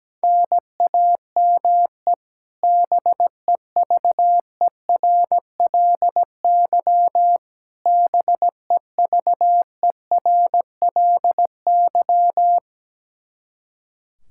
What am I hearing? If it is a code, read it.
Morse code, heard as a keyed tone, about 17 wpm, 700 Hz. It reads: NAME BEVERLY BEVERLY
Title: Practice QSO - Exchange 1 - Beginning - 17wpm